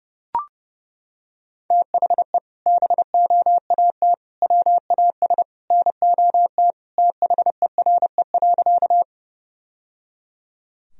Morse code, heard as a keyed tone, about 30 wpm, 700 Hz. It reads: THE BOAT WAS NOT THERE.
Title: Sentences from Top 1000 Words - Review Part 3 - 30wpm